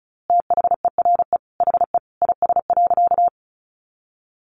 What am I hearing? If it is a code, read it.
Morse code, heard as a keyed tone, about 34 wpm, 700 Hz. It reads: THERE HE IS.
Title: A Sound Of Thunder 35wpm